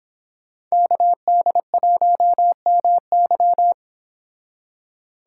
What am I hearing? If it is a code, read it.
Morse code, heard as a keyed tone, about 26 wpm, 700 Hz. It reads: KD1MY
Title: Speed Racing - USA 2x2 Call Signs 22wpm